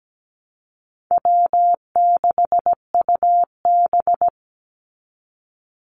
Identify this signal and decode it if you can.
Morse code, heard as a keyed tone, about 17 wpm, 700 Hz. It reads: W6UB
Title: NA SOTA Activator Callsigns 17wpm